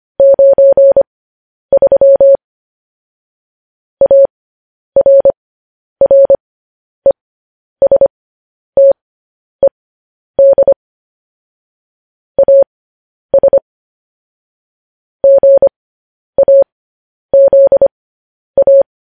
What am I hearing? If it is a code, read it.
Morse code, heard as a keyed tone, about 25 wpm, spaced as for 10 wpm, 550 Hz. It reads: 93 ARRESTED AS GAZA
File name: podcast / News headlines-1x-25x10